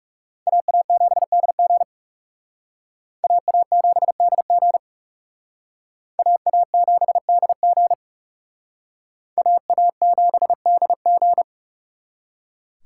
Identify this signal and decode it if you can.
Morse code, heard as a keyed tone, about 36 wpm, 700 Hz. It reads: AA7DG AA7DG AA7DG AA7DG
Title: Speed Racing - USA 2x2 Call Signs 30wpm